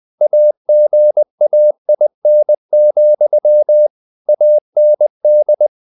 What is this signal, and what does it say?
Morse code, heard as a keyed tone, about 20 wpm, 600 Hz. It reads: AGAIN, AND
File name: the old man and the sea / the old man and the sea part 1 20wpm